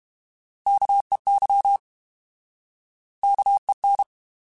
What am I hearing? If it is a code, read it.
Morse code, heard as a keyed tone, about 32 wpm, 800 Hz. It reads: KEY KEN
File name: Alphabetical / Alphabetical mp3s / Ks @30